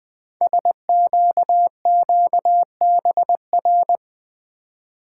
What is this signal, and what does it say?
Morse code, heard as a keyed tone, about 20 wpm, 700 Hz. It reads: SQQBR
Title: Code Groups - 5 Letters - 20wpm